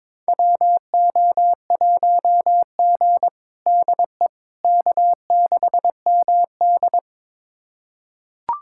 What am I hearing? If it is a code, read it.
Morse code, heard as a keyed tone, about 22 wpm, 700 Hz. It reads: WO1G DE K6MD
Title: USA Callsign DE Callsign - 4 Characters - 22wpm